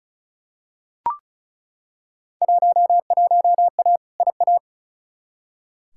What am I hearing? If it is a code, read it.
Morse code, heard as a keyed tone, about 36 wpm, 700 Hz. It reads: 11A IA
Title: Field Day Exchanges Encore - 35wpm